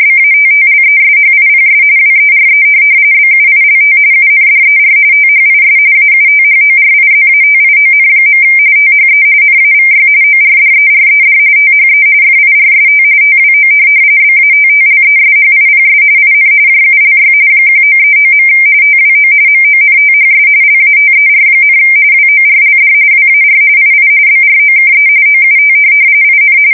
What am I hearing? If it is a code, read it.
RTTY, about 45.45 baud. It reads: BATTLESHIPS WERE BUILT UNDER THIS AUTHORIZATION. THE NAVY ORDERED THE FIRST TWO BATTLESHIPS IN JULY 1939. TWO MONTHS LATER, WORLD WAR II BROKE OUT IN
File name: fsk170 / USS-New-Jersey